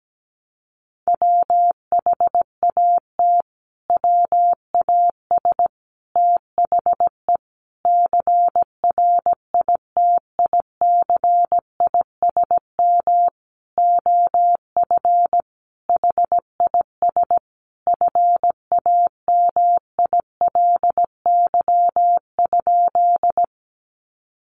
Morse code, heard as a keyed tone, about 17 wpm, 700 Hz. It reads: WHAT WAS THE CRITICISM OF HIS FAMILY?
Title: Sentences from Top 1300 Words 17wpm